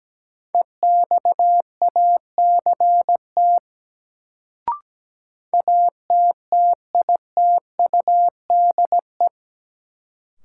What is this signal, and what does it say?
Morse code, heard as a keyed tone, about 17 wpm, 700 Hz. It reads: EXACT ATTITUDE
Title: Sentences from Top 1300 Words - New Word Drill Alt - 17wpm